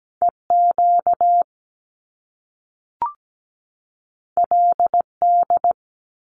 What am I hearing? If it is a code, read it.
Morse code, heard as a keyed tone, about 17 wpm, 700 Hz. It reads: EQ LD
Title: Top Two Letter Combinations in Words 17wpm